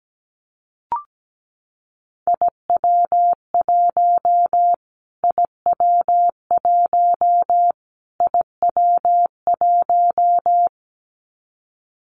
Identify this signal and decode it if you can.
Morse code, heard as a keyed tone, about 17 wpm, 700 Hz. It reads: IW1 IW1 IW1
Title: Callsign Prefixes for Top 25 DXCC - Repeated 3x - 17wpm